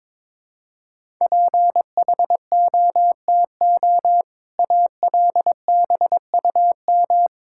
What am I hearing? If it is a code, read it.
Morse code, heard as a keyed tone, about 22 wpm, 690 Hz. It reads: PHOTO ALBUM
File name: Difficult Words with Hints - 22wpm